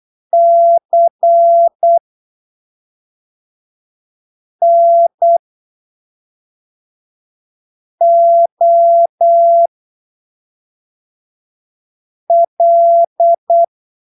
Morse code, heard as a keyed tone, about 8 wpm, 680 Hz. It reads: C N O L
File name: arquivos / texto / texto 24 08ppm